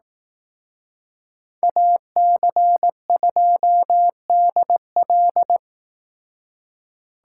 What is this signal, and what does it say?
Morse code, heard as a keyed tone, about 18 wpm, 700 Hz. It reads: AC2DL
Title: Speed Racing - USA Call Signs 15wpm